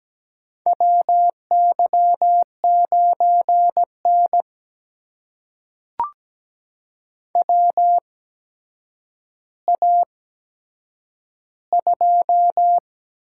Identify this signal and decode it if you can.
Morse code, heard as a keyed tone, about 17 wpm, spaced as for 8 wpm, 700 Hz. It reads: WY9N WA2
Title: USA Call Signs - 8x Character Spacing - 17wpm